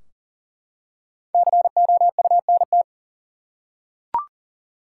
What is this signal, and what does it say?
Morse code, heard as a keyed tone, about 40 wpm, 700 Hz. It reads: COUNT
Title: Five Letter Words - 40wpm